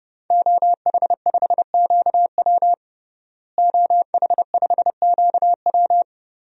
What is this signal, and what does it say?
Morse code, heard as a keyed tone, about 32 wpm, 700 Hz. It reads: OH5QW OH5QW
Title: DXCC-100 Call Signs - 5 Characters - Repeated 2x - 30wpm